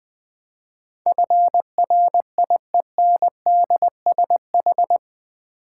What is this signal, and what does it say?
Morse code, heard as a keyed tone, about 20 wpm, 700 Hz. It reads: FRIENDSH
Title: Build A Word - 4 Letters At A Time - 10-Letter Words - 20wpm